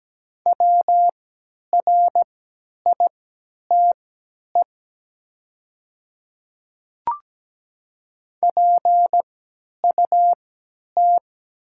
Morse code, heard as a keyed tone, about 17 wpm, spaced as for 10 wpm, 700 Hz. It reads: WRITE PUT
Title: 3x Character Spacing - Top 200 Words - 17wpm